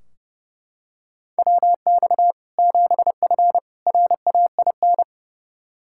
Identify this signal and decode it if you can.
Morse code, heard as a keyed tone, about 30 wpm, 700 Hz. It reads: WX 7F RAIN
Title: Practice QSO - Exchange 2 - Info - 30wpm